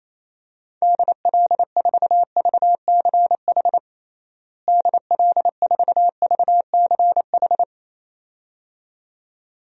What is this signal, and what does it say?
Morse code, heard as a keyed tone, about 28 wpm, 700 Hz. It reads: DL4VCH DL4VCH
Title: DXCC-100 Call Signs - 6 Characters - Repeated 2x - 28wpm